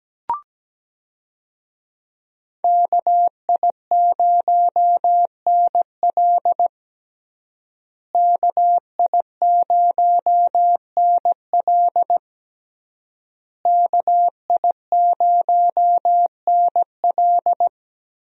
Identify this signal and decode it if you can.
Morse code, heard as a keyed tone, about 17 wpm, 700 Hz. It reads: KI0NL KI0NL KI0NL
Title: USA Call Signs - 5 Characters - Repeated 3x - 17wpm